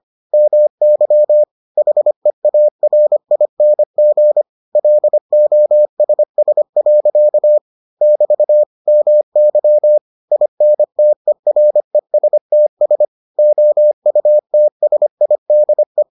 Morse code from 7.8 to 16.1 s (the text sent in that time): <BT> MY INTERESTS OUTSIDE